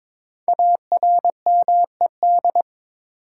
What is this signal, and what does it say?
Morse code, heard as a keyed tone, about 22 wpm, 700 Hz. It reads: ARMED
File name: 5 Letter Words - First 3 Letters Given - 22wpm